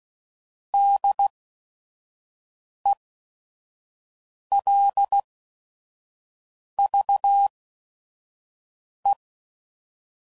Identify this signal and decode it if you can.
Morse code, heard as a keyed tone, about 16 wpm, spaced as for 5 wpm, 800 Hz. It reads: DELVE
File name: Prefixes / By First Letter / Ds